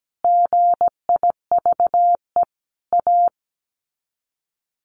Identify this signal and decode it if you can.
Morse code, heard as a keyed tone, about 17 wpm, 700 Hz. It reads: GIVE A
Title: Sets of 5 Words Encore2 17wpm